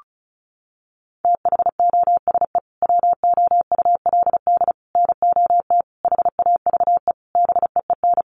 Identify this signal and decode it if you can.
Morse code, heard as a keyed tone, about 34 wpm, 700 Hz. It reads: THOSE WOULD NOT HAVE BEEN
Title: Sentences from Top 1300 Words 35wpm